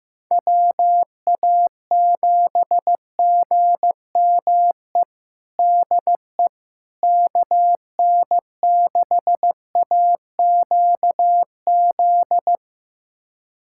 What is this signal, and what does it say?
Morse code, heard as a keyed tone, about 16 wpm, 700 Hz. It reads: WA7GME DE KN6AQZ
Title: USA Callsign DE Callsign 15wpm